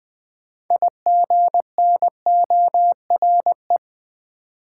Morse code, heard as a keyed tone, about 20 wpm, 700 Hz. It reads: IGNORE
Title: Build A Word - 2 Letters At A Time - 6-Letter Words - 20wpm